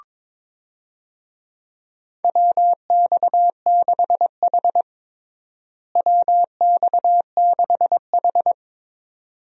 Morse code, heard as a keyed tone, about 21 wpm, 700 Hz. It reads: WX6H WX6H
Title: USA Call Signs - 4 Characters - Repeated 2x - 22wpm